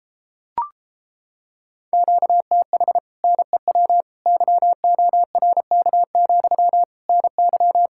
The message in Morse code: QTH NEW YORK, NY

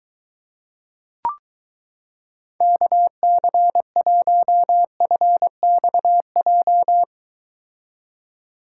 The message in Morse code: KC1FXJ

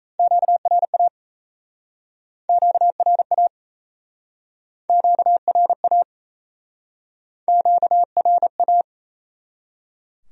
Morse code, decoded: QRA QRA QRA QRA

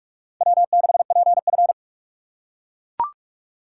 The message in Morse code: W6PF

